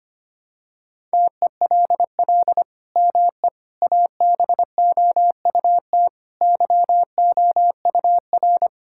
Morse code, heard as a keyed tone, about 25 wpm, 700 Hz. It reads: TELL ME ABOUT YOUR
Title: Sentences from Top 1600 Words 25wpm